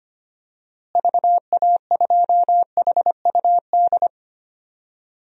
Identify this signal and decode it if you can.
Morse code, heard as a keyed tone, about 25 wpm, 700 Hz. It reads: VA2HUD
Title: DXCC-100 Call Signs - 6 Characters - Repeated 2x - 25wpm